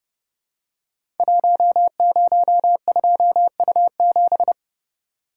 Morse code, heard as a keyed tone, about 30 wpm, 700 Hz. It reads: 102U7
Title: Code Groups - 5 Letter:Numbers - 30wpm